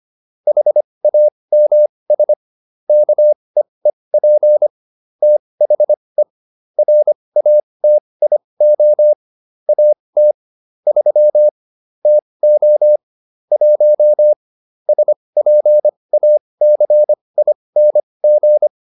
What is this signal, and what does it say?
Morse code, heard as a keyed tone, about 25 wpm, spaced as for 20 wpm, 600 Hz. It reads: HAMS KEEP THE RATIO AT 3 TO 1 SPACING